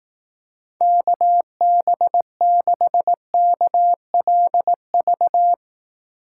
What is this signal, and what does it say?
Morse code, heard as a keyed tone, about 18 wpm, 700 Hz. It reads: KB6KLV
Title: Speed Racing - USA Call Signs 15wpm